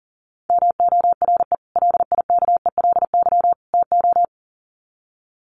Morse code, heard as a keyed tone, about 40 wpm, 700 Hz. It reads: MORE LIKELY TO
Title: Sets of 3 Words 40wpm